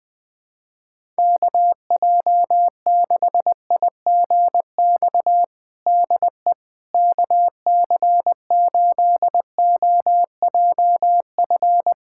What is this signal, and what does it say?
Morse code, heard as a keyed tone, about 21 wpm, 700 Hz. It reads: KJ6IGX DE KC8OJF
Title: USA Callsign DE Callsign - 6 Characters - 20wpm